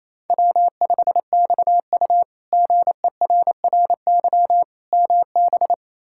Morse code, heard as a keyed tone, about 28 wpm, 700 Hz. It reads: W5XU GERRY MB
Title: K1USN SST QSO Practice - 28wpm